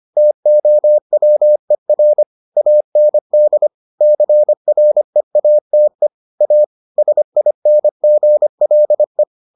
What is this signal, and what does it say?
Morse code, heard as a keyed tone, about 25 wpm, 600 Hz. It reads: TOWER AND CREATE A SINGLE